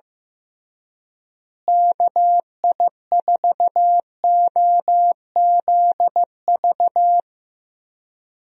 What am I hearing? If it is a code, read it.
Morse code, heard as a keyed tone, about 15 wpm, 700 Hz. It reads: KI4OZV
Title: USA Call Signs - 6 Characters - Repeated 3x - 15wpm